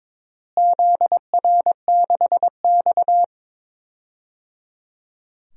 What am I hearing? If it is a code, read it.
Morse code, heard as a keyed tone, about 21 wpm, 700 Hz. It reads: ZR6X
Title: DXCC-100 Call Signs - 4 Characters - Repeated 3x - 22wpm